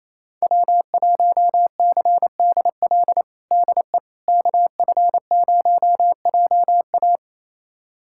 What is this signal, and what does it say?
Morse code, heard as a keyed tone, about 28 wpm, 700 Hz. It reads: W1CDL DE KF0JA
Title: USA Callsign DE Callsign - 5 Characters - 28wpm